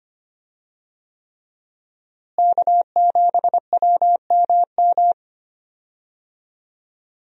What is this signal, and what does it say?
Morse code, heard as a keyed tone, about 25 wpm, 700 Hz. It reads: K7WMM
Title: USA Call Signs - 5 Characters - Repeated 3x - 25wpm